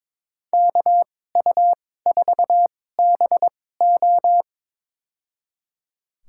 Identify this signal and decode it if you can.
Morse code, heard as a keyed tone, about 22 wpm, spaced as for 16 wpm, 700 Hz. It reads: KU4BO
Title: USA Call Signs - 5 Characters - 2x Character Spacing - 22wpm